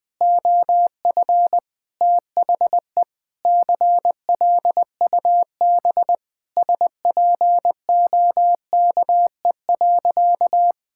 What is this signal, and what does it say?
Morse code, heard as a keyed tone, about 20 wpm, 700 Hz. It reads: OF THE CLUB SPOKE.
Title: Sentences with 4-Word Phrases - Spoken Beginning - 20wpm